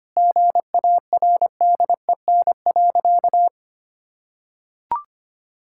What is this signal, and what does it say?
Morse code, heard as a keyed tone, about 25 wpm, 700 Hz. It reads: GARDEN.
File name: practice / 25wpm / Sentences with 3-Word Phrases - 25wpm